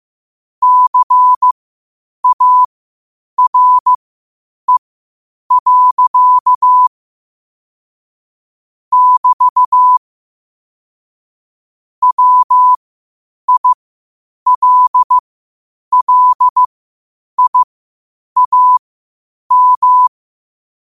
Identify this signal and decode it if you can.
Morse code, heard as a keyed tone, about 15 wpm, spaced as for 8 wpm, 1000 Hz. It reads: CARE. <BT> WILLIAM
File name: qotdfiles / 8wpm